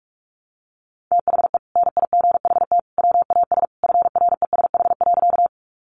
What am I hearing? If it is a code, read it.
Morse code, heard as a keyed tone, about 44 wpm, 700 Hz. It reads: THE NIGHT WAS FRESH.
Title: Sentences from Top 1200 Words 45wpm